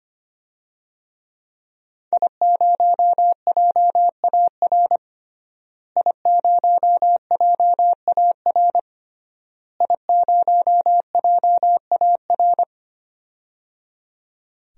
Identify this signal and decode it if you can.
Morse code, heard as a keyed tone, about 25 wpm, 700 Hz. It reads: I0JAR I0JAR I0JAR
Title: DXCC-100 Call Signs - 5 Characters - Repeated 3x - 25wpm